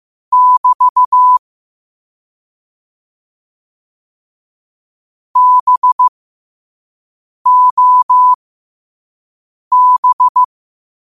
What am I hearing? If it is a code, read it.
Morse code, heard as a keyed tone, about 15 wpm, spaced as for 5 wpm, 1000 Hz. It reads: <BT> BOB
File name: qotdfiles / 5wpm